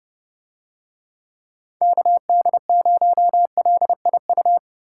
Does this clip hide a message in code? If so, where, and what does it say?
Yes: Morse code from 1.8 to 4.6 s, heard as a keyed tone, about 30 wpm, 700 Hz. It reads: KD0LIU